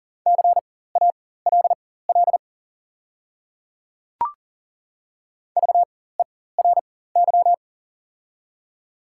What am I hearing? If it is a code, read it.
Morse code, heard as a keyed tone, about 38 wpm, spaced as for 18 wpm, 700 Hz. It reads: CALL VERY
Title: 4x Character Spacing - Top 200 Words - 40wpm